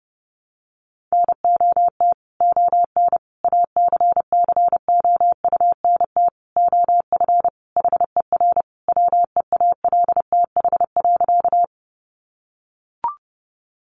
Morse code, heard as a keyed tone, about 28 wpm, 700 Hz. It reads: NOT ON ACCOUNT OF HER WEALTH.
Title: Sentences from Top 1200 Words 30wpm